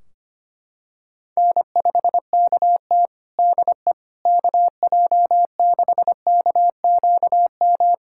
Morse code, heard as a keyed tone, about 25 wpm, 700 Hz. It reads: N5KT DE KJ6KQM
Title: Practice QSO - Exchange 4 - Ending - 25wpm